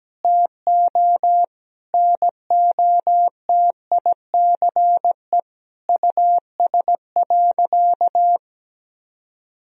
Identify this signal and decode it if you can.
Morse code, heard as a keyed tone, about 17 wpm, 700 Hz. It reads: TO NOTICE US.
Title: Sentences from Top 1000 Words - Review Part 2 - 17wpm